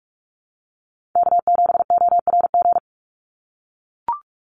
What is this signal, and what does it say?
Morse code, heard as a keyed tone, about 44 wpm, 700 Hz. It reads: K7ORG
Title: NA SOTA Chaser Callsigns 45wpm